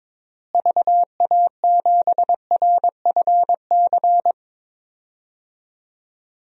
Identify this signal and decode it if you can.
Morse code, heard as a keyed tone, about 22 wpm, 700 Hz. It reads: VA7RFC
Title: DXCC-100 Call Signs - 6 Characters - Repeated 3x - 22wpm